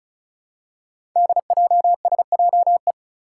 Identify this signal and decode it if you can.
Morse code, heard as a keyed tone, about 36 wpm, 700 Hz. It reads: DJSJE
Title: Five Challenging Letter-Number Combinations 35wpm